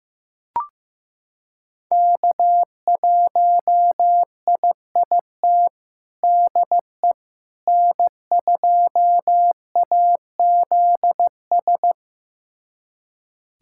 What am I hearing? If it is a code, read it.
Morse code, heard as a keyed tone, about 15 wpm, 700 Hz. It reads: K1IIT DE N2AZS
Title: USA Callsign DE Callsign - 5 Characters - 15wpm